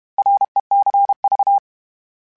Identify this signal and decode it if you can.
Morse code, heard as a keyed tone, about 32 wpm, 800 Hz. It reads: RECV
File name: Abreviations / Abreviations mp3s / Abreviations@30